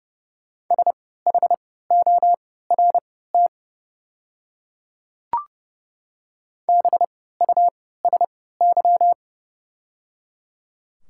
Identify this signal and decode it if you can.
Morse code, heard as a keyed tone, about 30 wpm, spaced as for 17 wpm, 700 Hz. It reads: SHORT BUSY
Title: 3x Character Spacing - Top 500 Words - 30wpm